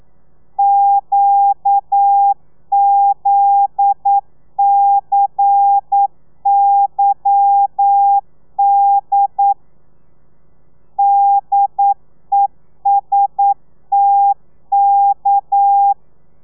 Morse code, heard as a keyed tone, about 9 wpm, 800 Hz. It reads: QZCYD DESTK